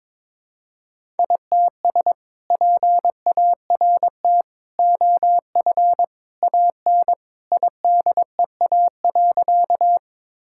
Morse code, heard as a keyed tone, about 22 wpm, 700 Hz. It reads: ITS PART OF AN IDEA.
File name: practice / 22wpm / Sentences with 2-Word Phrases - 22wpm